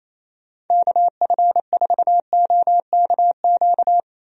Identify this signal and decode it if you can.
Morse code, heard as a keyed tone, about 27 wpm, 700 Hz. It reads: KF4OKQ
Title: USA Call Signs - 6 Characters - Repeated 3x - 28wpm